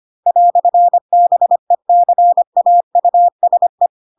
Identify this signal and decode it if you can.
Morse code, heard as a keyed tone, about 25 wpm, 700 Hz. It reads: "BECAUSE
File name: Pinocchio / Pinocchio ch4 25 25 ik1hge